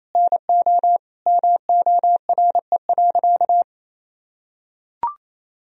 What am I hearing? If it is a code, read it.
Morse code, heard as a keyed tone, about 28 wpm, 700 Hz. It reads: NO MORE.
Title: Sentences from Top 1000 Words - Review Part 1 - 28wpm